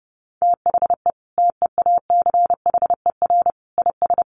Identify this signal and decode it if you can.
Morse code, heard as a keyed tone, about 30 wpm, 700 Hz. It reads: THE TEACHER IS
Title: Easy Phrases 30wpm